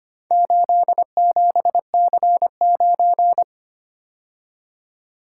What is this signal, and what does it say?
Morse code, heard as a keyed tone, about 25 wpm, 700 Hz. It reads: 87C9
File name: practice / 25wpm / Four Letter-Number Combinations - 25wpm